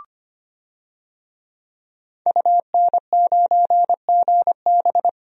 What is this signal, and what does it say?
Morse code, heard as a keyed tone, about 25 wpm, 700 Hz. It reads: UN9GB
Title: DXCC-100 Call Signs - 5 Characters - Repeated 2x - 25wpm